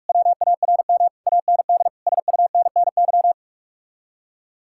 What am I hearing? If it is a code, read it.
Morse code, heard as a keyed tone, about 59 wpm, spaced as for 52 wpm, 700 Hz. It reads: WARM AND SUNNY
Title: Speed-Racing Common QSO Phrases - 45wpm